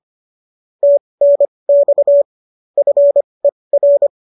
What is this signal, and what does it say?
Morse code, heard as a keyed tone, about 25 wpm, spaced as for 20 wpm, 570 Hz. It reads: TNX FER
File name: qso221 20